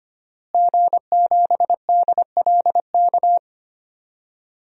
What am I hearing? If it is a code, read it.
Morse code, heard as a keyed tone, about 25 wpm, 700 Hz. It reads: G7DLK
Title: DXCC-100 Call Signs - 5 Characters - Repeated 2x - 25wpm